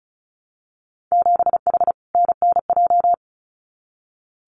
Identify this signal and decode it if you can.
Morse code, heard as a keyed tone, about 34 wpm, 700 Hz. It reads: 7H NNJ
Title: Winter Field Day Exchanges Encore 35wpm